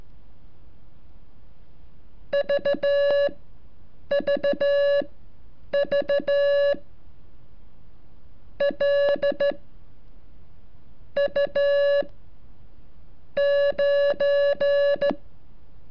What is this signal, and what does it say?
Morse code, heard as a keyed tone, about 13 wpm, spaced as for 8 wpm, 570 Hz. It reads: VVV L U 9
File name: curso / ejercicio 35